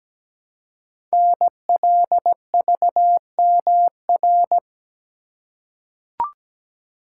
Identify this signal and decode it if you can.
Morse code, heard as a keyed tone, about 17 wpm, 700 Hz. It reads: NLVMR
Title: Code Groups - 5 Letters - 17wpm